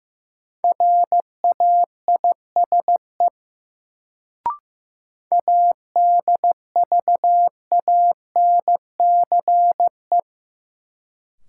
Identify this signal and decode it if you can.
Morse code, heard as a keyed tone, about 15 wpm, 700 Hz. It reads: RAISE ADVANCE
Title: Sentences from Top 1400 Words - New Word Drill - 15wpm